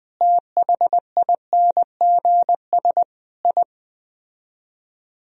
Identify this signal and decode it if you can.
Morse code, heard as a keyed tone, about 20 wpm, 700 Hz. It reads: THINGS I